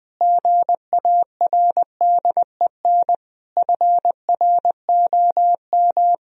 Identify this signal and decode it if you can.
Morse code, heard as a keyed tone, about 21 wpm, 700 Hz. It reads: GARDEN FROM
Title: Sentences from Top 1000 Words - Review Part 4 - 20wpm